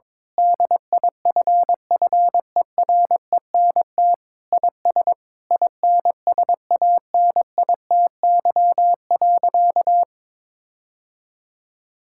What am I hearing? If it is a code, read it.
Morse code, heard as a keyed tone, about 22 wpm, 700 Hz. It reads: DIFFERENT IS INSANITY.
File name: practice / 22wpm / Sentences with 8-Word Phrases - 22wpm